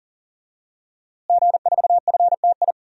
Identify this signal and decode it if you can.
Morse code, heard as a keyed tone, about 40 wpm, 700 Hz. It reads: G4FTI
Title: DXCC-100 Call Signs - 5 Characters - Repeated 3x - 40wpm